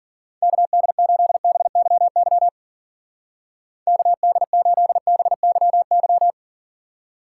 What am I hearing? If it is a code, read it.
Morse code, heard as a keyed tone, about 50 wpm, 700 Hz. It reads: KD8BYY KD8BYY
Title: Speed Racing - USA 2x3 Call Signs 40wpm